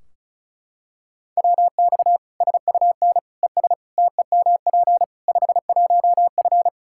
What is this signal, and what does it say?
Morse code, heard as a keyed tone, about 36 wpm, 700 Hz. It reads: WX SUN ES TEMP 51F
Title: Practice QSO - Exchange 2 - Info - 35wpm